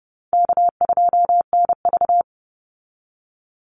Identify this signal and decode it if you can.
Morse code, heard as a keyed tone, about 30 wpm, 700 Hz. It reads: K2NV
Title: Easy Call Signs - 30wpm